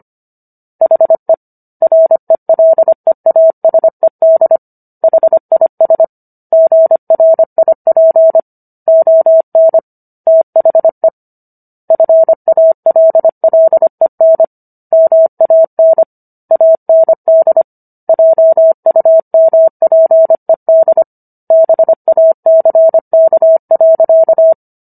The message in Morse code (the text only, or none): HE RELEASED HIS GRIP ON THE FALLEN MAN AND JUMPED BACK.